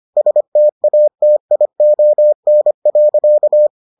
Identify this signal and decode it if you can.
Morse code, heard as a keyed tone, about 25 wpm, 600 Hz. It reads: STATION.